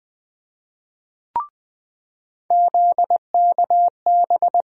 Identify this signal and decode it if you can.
Morse code, heard as a keyed tone, about 20 wpm, 700 Hz. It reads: ZKB
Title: USA Call Sign Postfixes - 3 Letters 20wpm